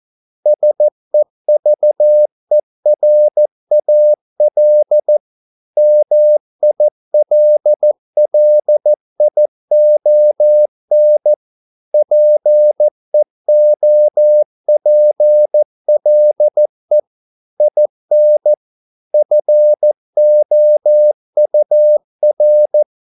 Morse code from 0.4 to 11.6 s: SEVERAL MILLION

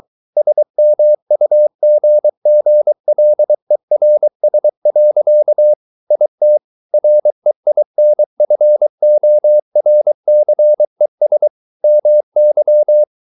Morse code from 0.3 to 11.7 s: SMUGGLERS. IT REINFORCES

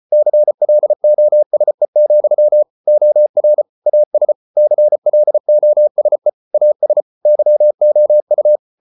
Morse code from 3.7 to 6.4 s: AS CLOSE

